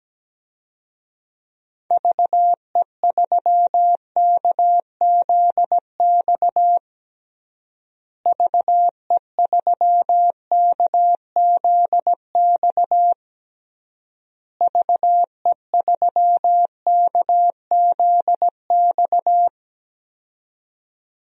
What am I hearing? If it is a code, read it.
Morse code, heard as a keyed tone, about 17 wpm, 700 Hz. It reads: VE3KZX VE3KZX VE3KZX
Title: DXCC-100 Call Signs - 6 Characters - Repeated 3x - 17wpm